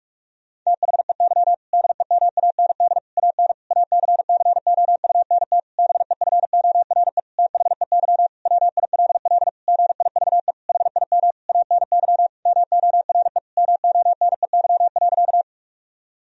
Morse code, heard as a keyed tone, about 46 wpm, 700 Hz. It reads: THEY DEMAND AN ACCOUNT BEFORE THEY WILL GIVE HIM ANY MORE MONEY.
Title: Sentences from Top 1000 Words - Review Part 4 - 45wpm